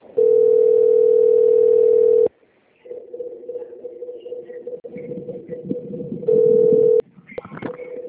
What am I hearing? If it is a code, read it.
Ringback tone (North America), 440 + 480 Hz.
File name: out- - - - - .28866